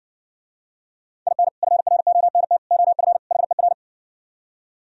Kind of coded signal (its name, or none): Morse code